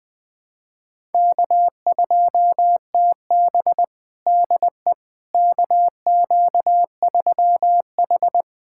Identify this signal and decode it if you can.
Morse code, heard as a keyed tone, about 21 wpm, 700 Hz. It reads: K2TB DE KQ3H
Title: USA Callsign DE Callsign - 4 Characters - 20wpm